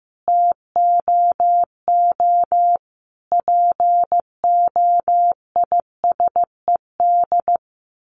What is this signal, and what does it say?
Morse code, heard as a keyed tone, about 15 wpm, 700 Hz. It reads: TOO POISED
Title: A Sound Of Thunder 15wpm